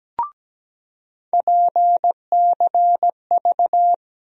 Morse code, heard as a keyed tone, about 17 wpm, 700 Hz. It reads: PCV
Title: USA Call Sign Postfixes - 3 Letters 17wpm